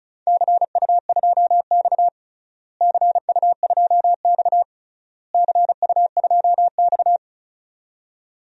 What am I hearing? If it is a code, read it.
Morse code, heard as a keyed tone, about 34 wpm, 700 Hz. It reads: CU2X CU2X CU2X
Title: DXCC-100 Call Signs - 4 Characters - Repeated 3x - 35wpm